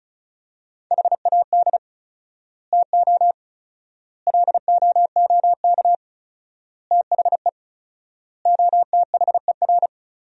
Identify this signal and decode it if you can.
Morse code, heard as a keyed tone, about 36 wpm, 700 Hz. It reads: HAD TO LOOK THE OTHER
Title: 4x Word-Spacing - Sentences from Top 100 Words - 35wpm